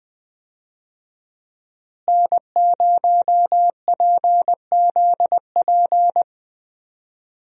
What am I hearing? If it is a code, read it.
Morse code, heard as a keyed tone, about 21 wpm, 700 Hz. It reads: N0PZP
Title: USA Call Signs - 5 Characters - Repeated 3x - 20wpm